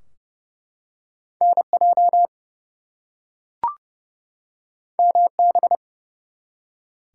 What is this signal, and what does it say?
Morse code, heard as a keyed tone, about 30 wpm, 700 Hz. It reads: NJ MB
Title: US States and Canadian Provinces 30wpm